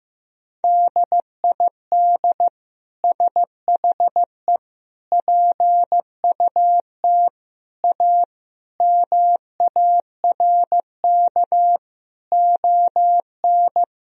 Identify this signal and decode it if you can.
Morse code, heard as a keyed tone, about 15 wpm, 700 Hz. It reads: DID SHE PUT A MARK ON
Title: Sentences from Top 1000 Words - Review Part 1 - 15wpm